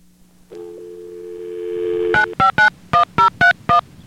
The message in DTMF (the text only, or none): 9591031